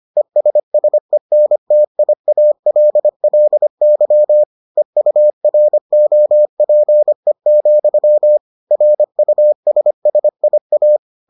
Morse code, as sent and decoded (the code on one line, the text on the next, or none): . ... ... -. - .. .- .-.. .-.. -.-- / . ..- .-. --- .--. --..-- / .-. ..- ... ... .. .-
ESSENTIALLY EUROPE, RUSSIA